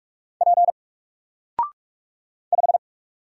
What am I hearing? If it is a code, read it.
Morse code, heard as a keyed tone, about 44 wpm, 700 Hz. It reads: P 5
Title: Single Letter-Number 45wpm